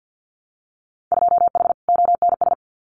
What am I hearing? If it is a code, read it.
Morse code, heard as a keyed tone, about 50 wpm, 700 Hz. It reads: 2H ONS
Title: Winter Field Day Exchanges Encore 50wpm